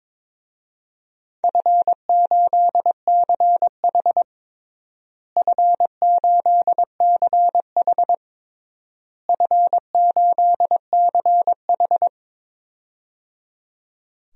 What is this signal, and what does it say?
Morse code, heard as a keyed tone, about 22 wpm, 700 Hz. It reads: F8CH F8CH F8CH
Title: DXCC-100 Call Signs - 4 Characters - Repeated 3x - 22wpm